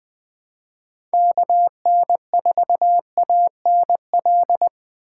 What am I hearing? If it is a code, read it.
Morse code, heard as a keyed tone, about 20 wpm, 700 Hz. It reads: KN4ANL
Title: Speed Racing - USA 2x3 Call Signs 17wpm